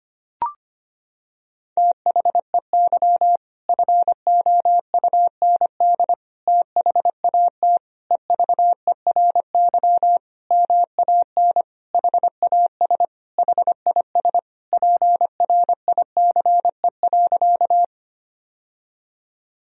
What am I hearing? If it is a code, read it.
Morse code, heard as a keyed tone, about 25 wpm, 700 Hz. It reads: THEY FOUND THAT EVERY MAN HAS HIS PRICE.
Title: Sentences from Top 1000 Words - Review Part 4 - 25wpm